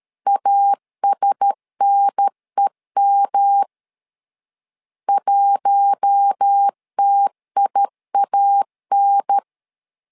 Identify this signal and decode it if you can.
Morse code, heard as a keyed tone, about 13 wpm, 800 Hz. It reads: ASNEM 1TIAN
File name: LEZIONI / 4 Q1S 12wpm 1802